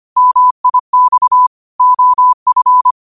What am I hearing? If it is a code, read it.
Morse code, heard as a keyed tone, about 25 wpm, 1000 Hz. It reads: MIX OF